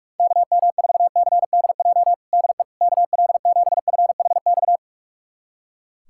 Morse code, heard as a keyed tone, about 41 wpm, 700 Hz. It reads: KM4CDJ DE KL7FHX
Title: USA Callsign DE Callsign - 6 Characters - 45wpm